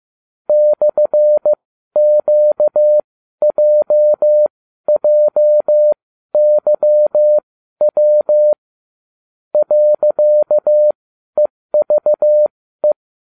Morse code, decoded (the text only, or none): /QJJYW .EVE